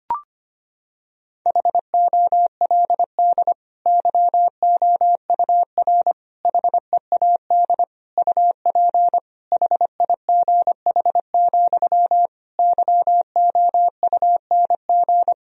Morse code, heard as a keyed tone, about 25 wpm, 700 Hz. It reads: HOLD YOUR HEAD UP HIGH, YOUNG
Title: Sentences from Top 1000 Words - Review Part 2 - 25wpm